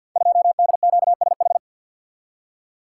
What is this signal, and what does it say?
Morse code, heard as a keyed tone, about 48 wpm, 700 Hz. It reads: 2DQSH
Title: Code Groups - 5 Letter:Numbers - 50wpm